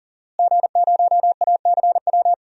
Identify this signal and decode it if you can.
Morse code, heard as a keyed tone, about 39 wpm, 700 Hz. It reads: G0ACW
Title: DXCC-100 Call Signs - 5 Characters - Repeated 3x - 40wpm